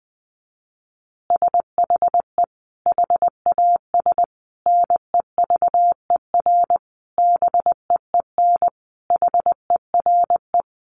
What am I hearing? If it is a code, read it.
Morse code, heard as a keyed tone, about 20 wpm, 700 Hz. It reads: SHE HAS NEVER BEEN HERE
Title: Additional Easy Phrases 20wpm